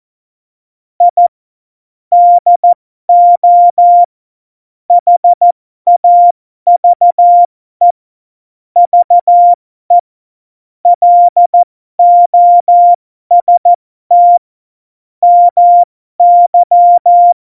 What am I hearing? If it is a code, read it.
Morse code, heard as a keyed tone, about 14 wpm, spaced as for 12 wpm, 700 Hz. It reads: I DO HAVE VE LOST MY
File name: Morsekurs / Witze